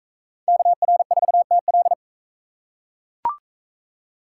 Morse code, heard as a keyed tone, about 39 wpm, 700 Hz. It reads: KR4TL